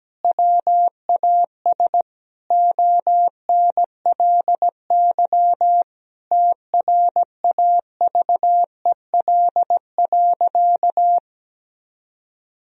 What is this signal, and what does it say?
Morse code, heard as a keyed tone, about 17 wpm, 700 Hz. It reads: WAS ONLY TRAVEL.
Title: Sentences from Top 1000 Words - Review Part 2 - 17wpm